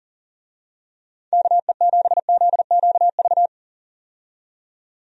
Morse code, heard as a keyed tone, about 40 wpm, 700 Hz. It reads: KE7ZQV
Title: USA Call Signs - 6 Characters - 40wpm